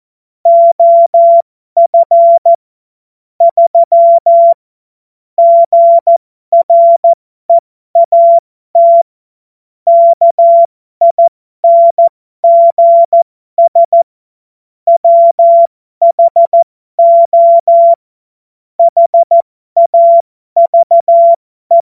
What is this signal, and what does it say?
Morse code, heard as a keyed tone, about 14 wpm, spaced as for 12 wpm, 680 Hz. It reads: OF 3 GREAT KINGS WHO HAVE
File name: Morsekurs / Witze